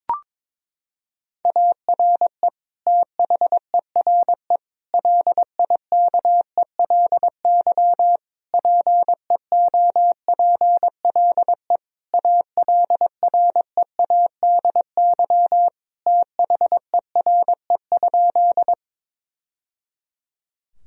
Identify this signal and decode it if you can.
Morse code, heard as a keyed tone, about 21 wpm, 700 Hz. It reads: ARE THERE LIKELY PEOPLE ALREADY THERE?
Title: Sentences from Top 1000 Words - Review Part 4 - 22wpm